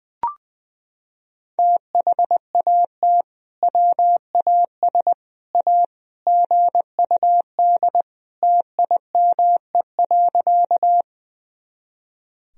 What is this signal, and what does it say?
Morse code, heard as a keyed tone, about 21 wpm, 700 Hz. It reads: THAT WAS A GUD TIME.
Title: Sentences from Top 200 Words and Abbreviations 20wpm